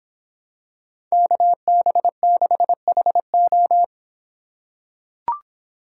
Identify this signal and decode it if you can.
Morse code, heard as a keyed tone, about 26 wpm, 700 Hz. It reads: KB6HO